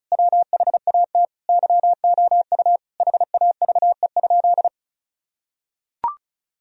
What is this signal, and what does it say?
Morse code, heard as a keyed tone, about 36 wpm, 700 Hz. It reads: WHAT YOU HAVE?
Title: Sentences from Top 1000 Words - Review Part 2 - 35wpm